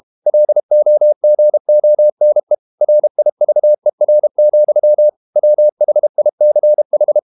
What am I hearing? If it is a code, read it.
Morse code, heard as a keyed tone, about 32 wpm, 600 Hz. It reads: LOGONE RIVER, WHICH
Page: Never 31wpm | CW med Gnister